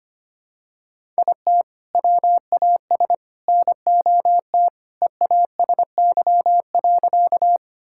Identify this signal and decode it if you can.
Morse code, heard as a keyed tone, about 25 wpm, 700 Hz. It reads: IT WAS NOT EASY.
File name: practice / 25wpm / Sentences with 3-Word Phrases - 25wpm